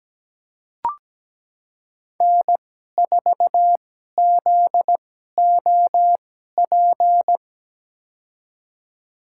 Morse code, heard as a keyed tone, about 17 wpm, spaced as for 12 wpm, 700 Hz. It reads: N4ZOP
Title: USA Call Signs - 5 Characters - 2x Character Spacing - 17wpm